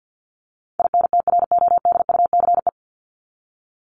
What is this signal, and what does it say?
Morse code, heard as a keyed tone, about 48 wpm, 700 Hz. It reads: INTRODUCE
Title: Top 1501-1750 Words 50wpm